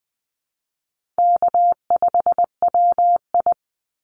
Morse code, heard as a keyed tone, about 20 wpm, 700 Hz. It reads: K5WI
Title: NA SOTA Chaser Callsigns 20wpm